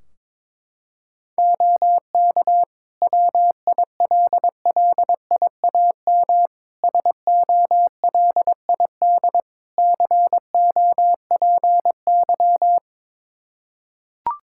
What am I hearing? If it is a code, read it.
Morse code, heard as a keyed tone, about 22 wpm, 700 Hz. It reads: OK WILLIAM SOLID COPY
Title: Practice QSO - Exchange 3 - More Info - 22wpm